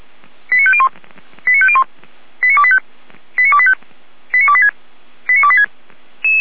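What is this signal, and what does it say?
Five-tone selective call (ZVEI), heard as quick runs of five tones, tones of about 70 ms. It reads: 89501 89501 89285 89285 89286 89286
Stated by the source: ZVEI five-tone selective call received on an SDR.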